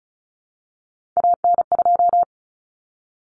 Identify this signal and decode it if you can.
Morse code, heard as a keyed tone, about 34 wpm, 700 Hz. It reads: AN2
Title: Common Cut Numbers - 3-Digit 35wpm